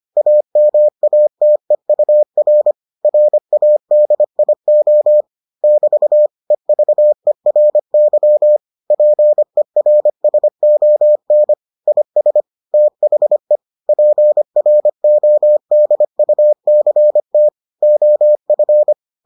Morse code, heard as a keyed tone, about 25 wpm, 600 Hz. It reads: AMATEUR RADIO <BT> EVERY PERSON IS THE PRODUCT OF